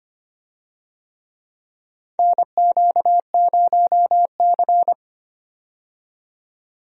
Morse code, heard as a keyed tone, about 25 wpm, 700 Hz. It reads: NQ0C
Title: USA Call Signs - 4 Characters - 25wpm